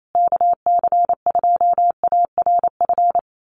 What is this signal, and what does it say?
Morse code, heard as a keyed tone, about 28 wpm, 700 Hz. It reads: KC2ARF
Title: USA Call Signs Repeated 2x 28wpm